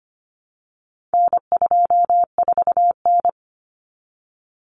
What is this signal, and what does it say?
Morse code, heard as a keyed tone, about 25 wpm, 700 Hz. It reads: N24N
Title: Common Cut Numbers - 4-Digit 25wpm